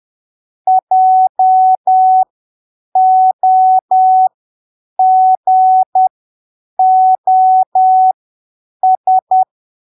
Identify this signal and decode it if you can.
Morse code, heard as a keyed tone, about 10 wpm, spaced as for 7 wpm, 750 Hz. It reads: JOGOS
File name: arquivos / texto / texto 08 10ppm